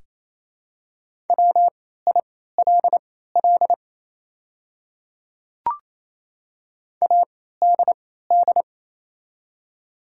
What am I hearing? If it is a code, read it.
Morse code, heard as a keyed tone, about 28 wpm, spaced as for 16 wpm, 700 Hz. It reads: WILL ADD
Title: 3x Character Spacing - Top 200 Words - 28wpm